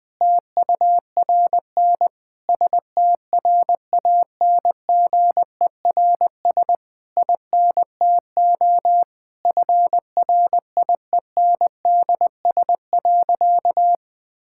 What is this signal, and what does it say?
Morse code, heard as a keyed tone, about 20 wpm, 700 Hz. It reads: TURN STRANGERS INTO FRIENDS.
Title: Difficult Sentences with Hints - 20wpm